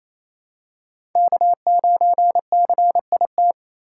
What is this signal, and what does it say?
Morse code, heard as a keyed tone, about 28 wpm, 700 Hz. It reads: K9CIT
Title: USA Call Signs - 5 Characters - 2x Character Spacing - 28wpm